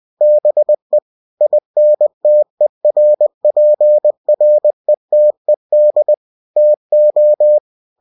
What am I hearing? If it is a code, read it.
Morse code, heard as a keyed tone, about 20 wpm, 600 Hz. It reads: BE INTERPRETED TO
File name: cw-books / winnie-the-pooh / winnie-20wpm-page-0119